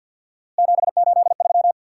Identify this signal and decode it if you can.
Morse code, heard as a keyed tone, about 50 wpm, 700 Hz. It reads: 783
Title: Worked All Europe DX Contest - QTC Exchange with Repeats - 50wpm